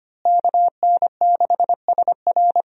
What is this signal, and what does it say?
Morse code, heard as a keyed tone, about 25 wpm, 700 Hz. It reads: KN6SR
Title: USA Call Signs - 5 Characters - Repeated 2x - 25wpm